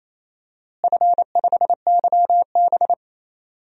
Morse code, heard as a keyed tone, about 28 wpm, 700 Hz. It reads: F5YB
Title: DXCC-100 Call Signs - 4 Characters - Repeated 3x - 28wpm